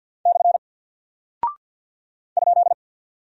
Morse code, heard as a keyed tone, about 48 wpm, 700 Hz. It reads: / ?
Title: Single Character (Letter:Number:Punctuation:Prosign) Alt 50wpm